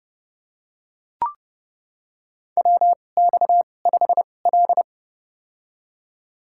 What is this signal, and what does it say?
Morse code, heard as a keyed tone, about 30 wpm, spaced as for 22 wpm, 700 Hz. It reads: WX5L
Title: USA Call Signs - 4 Characters - 2x Character Spacing - 30wpm